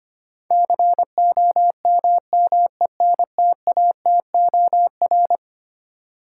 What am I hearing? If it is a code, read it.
Morse code, heard as a keyed tone, about 25 wpm, 700 Hz. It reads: COMMENTATOR
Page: morse-code-ninja-practice-25wpm